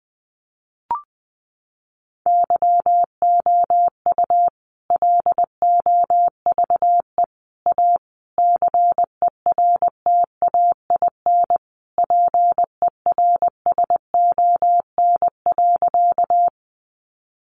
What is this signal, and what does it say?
Morse code, heard as a keyed tone, about 20 wpm, 700 Hz. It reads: YOU LOVE A CERTAIN PERSON.
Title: Sentences from Top 500 Words 20wpm